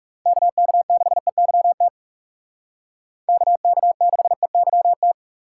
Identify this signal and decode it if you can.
Morse code, heard as a keyed tone, about 37 wpm, 700 Hz. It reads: KK6EYT KK6EYT
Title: Speed Racing - USA 2x3 Call Signs 30wpm